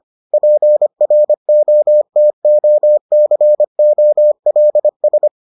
Morse code, heard as a keyed tone, about 25 wpm, 600 Hz. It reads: PROTOCOLS